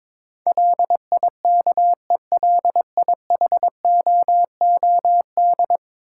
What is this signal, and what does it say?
Morse code, heard as a keyed tone, about 22 wpm, 700 Hz. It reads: LIKELIHOOD
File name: practice / 22wpm / 10 Letter Words - First 3 Letters Given - 22wpm